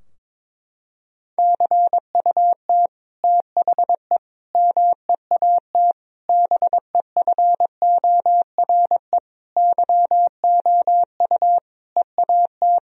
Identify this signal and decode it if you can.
Morse code, heard as a keyed tone, about 22 wpm, 700 Hz. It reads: CUT THE MEAT BEFORE YOU EAT
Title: Sentences from Top 1000 Words - Review Part 3 - 22wpm